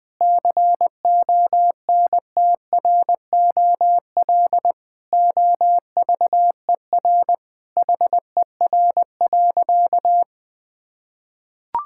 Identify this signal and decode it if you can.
Morse code, heard as a keyed tone, about 20 wpm, 700 Hz. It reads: CONTROL OVER HER.
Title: Sentences from Top 1000 Words - Review Part 5 - 20wpm